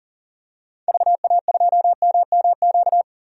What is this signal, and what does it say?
Morse code, heard as a keyed tone, about 40 wpm, 700 Hz. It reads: VA2MMQ
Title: DXCC-100 Call Signs - 6 Characters - 40wpm